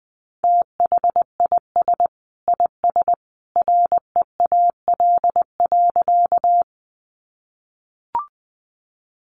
Morse code, heard as a keyed tone, about 20 wpm, 700 Hz. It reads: THIS IS REAL.
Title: Sentences from Top 300 Words 20wpm